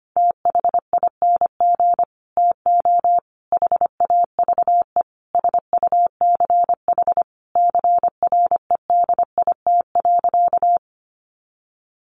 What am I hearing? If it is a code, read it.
Morse code, heard as a keyed tone, about 25 wpm, 700 Hz. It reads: THING TO HAVE SUCH CREDIT.
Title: Sentences from Top 1000 Words 25wpm